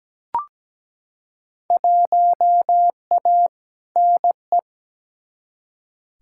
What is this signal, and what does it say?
Morse code, heard as a keyed tone, about 17 wpm, 700 Hz. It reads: 1A NE
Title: Field Day Exchanges Encore - 17wpm